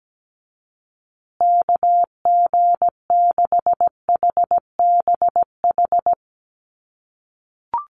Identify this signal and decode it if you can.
Morse code, heard as a keyed tone, about 17 wpm, 700 Hz. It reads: KG6HBH
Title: USA Call Signs Repeated 2x 17wpm